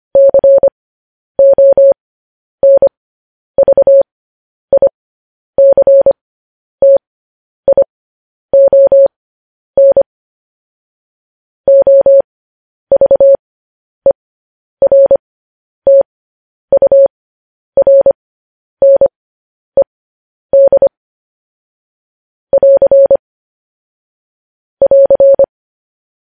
Morse code, heard as a keyed tone, about 25 wpm, spaced as for 10 wpm, 550 Hz. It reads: CONVICTION OVERTURNED <AR> <AR>
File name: podcast / News headlines-1x-25x10